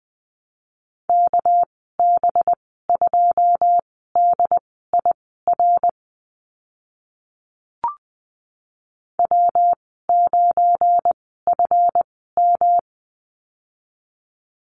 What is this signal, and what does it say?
Morse code, heard as a keyed tone, about 20 wpm, spaced as for 15 wpm, 700 Hz. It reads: KB2DIR W9FM
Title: 245 Lesson 39g - Letter z - Call Signs 20wpm